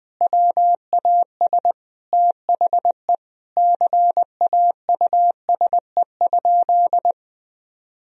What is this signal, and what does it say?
Morse code, heard as a keyed tone, about 20 wpm, 700 Hz. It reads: WAS THE CAUSE?
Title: Sentences from Top 1000 Words - Review Part 1 - 20wpm